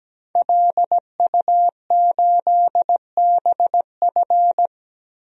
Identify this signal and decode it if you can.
Morse code, heard as a keyed tone, about 17 wpm, 700 Hz. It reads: LU8BF
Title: DXCC-100 Call Signs - 5 Characters - 17wpm